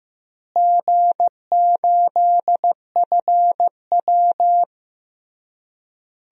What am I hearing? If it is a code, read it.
Morse code, heard as a keyed tone, about 15 wpm, 700 Hz. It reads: G8FW
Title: DXCC-100 Call Signs - 4 Characters - Repeated 3x - 15wpm